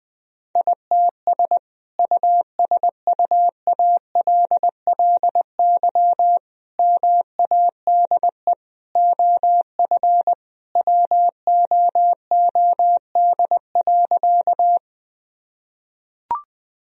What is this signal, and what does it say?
Morse code, heard as a keyed tone, about 21 wpm, 700 Hz. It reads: ITS USUALLY MADE OF WOOD.
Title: Sentences from Top 1000 Words - Review Part 3 - 20wpm